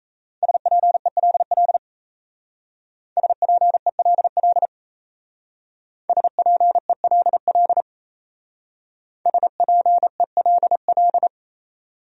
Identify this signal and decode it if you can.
Morse code, heard as a keyed tone, about 32 wpm, 700 Hz. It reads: SPELL SPELL SPELL SPELL
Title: Speed-Racing Top 200 Words 28wpm